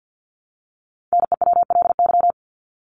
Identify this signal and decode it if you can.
Morse code, heard as a keyed tone, about 48 wpm, 700 Hz. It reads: NEWLY
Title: LY Suffix - Top 100 Words with Base Words 50wpm